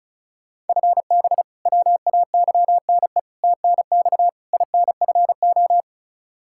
Morse code, heard as a keyed tone, about 36 wpm, 700 Hz. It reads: FB WAYNE TNX INFO
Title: Practice QSO - Exchange 3 - More Info - 35wpm